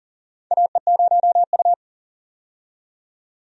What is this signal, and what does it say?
Morse code, heard as a keyed tone, about 40 wpm, 690 Hz. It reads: AE0U
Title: USA Call Signs - 4 Characters - 40wpm